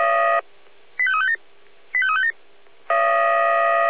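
Five-tone selective call (ZVEI), heard as quick runs of five tones, tones of about 70 ms. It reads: 85357 85357
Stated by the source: ZVEI five-tone selective call received on SDR.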